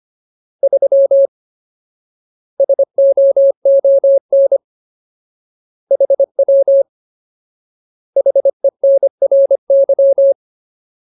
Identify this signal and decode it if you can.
Morse code, heard as a keyed tone, about 25 wpm, 550 Hz. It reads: 3 SOON HW HENRY
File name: Ultimate Word List / 25WPM Ultimate